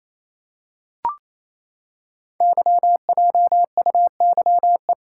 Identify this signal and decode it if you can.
Morse code, heard as a keyed tone, about 28 wpm, 700 Hz. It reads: YJUYE
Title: Code Groups - 5 Letters - 28wpm